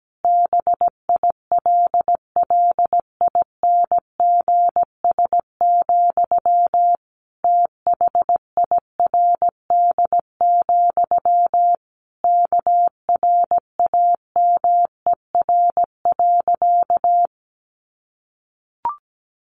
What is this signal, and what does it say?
Morse code, heard as a keyed tone, about 17 wpm, 700 Hz. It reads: BILLINGS, THIRD, KRAMER.
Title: A Sound Of Thunder 17wpm